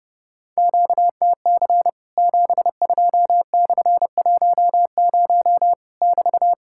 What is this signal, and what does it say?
Morse code, heard as a keyed tone, about 30 wpm, 700 Hz. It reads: QTC 72/10 <BT>
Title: Worked All Europe DX Contest - QTC Exchange - 30wpm